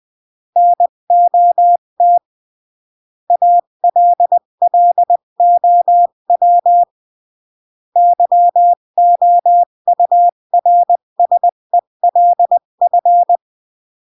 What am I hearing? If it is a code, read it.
Morse code, heard as a keyed tone, about 20 wpm, 700 Hz. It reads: NOT ALLOW YOURSELF
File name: CW / jodorowsky / 82 commandments 20-15